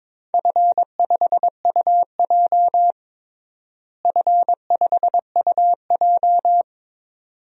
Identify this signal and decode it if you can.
Morse code, heard as a keyed tone, about 22 wpm, 700 Hz. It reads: F5UJ F5UJ
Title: DXCC-100 Call Signs - 4 Characters - Repeated 2x - 22wpm